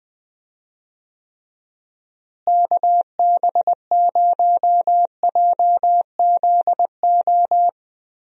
Morse code, heard as a keyed tone, about 20 wpm, 700 Hz. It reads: KB0JZO
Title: USA Call Signs - 6 Characters - Repeated 2x - 20wpm